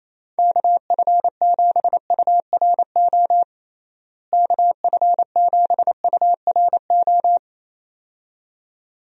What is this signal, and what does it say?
Morse code, heard as a keyed tone, about 28 wpm, 700 Hz. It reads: KF7URO KF7URO
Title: USA Call Signs - 6 Characters - Repeated 2x - 28wpm